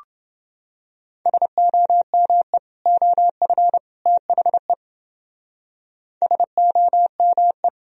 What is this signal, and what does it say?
Morse code, heard as a keyed tone, about 29 wpm, 700 Hz. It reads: SOME OF THE SOME